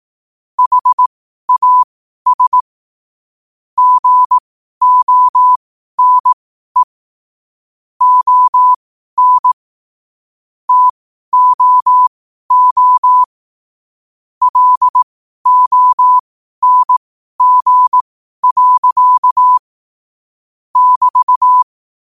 Morse code, heard as a keyed tone, about 18 wpm, spaced as for 12 wpm, 1000 Hz. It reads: HAS GONE ON TOO LONG. <BT>
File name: qotdfiles / 12wpm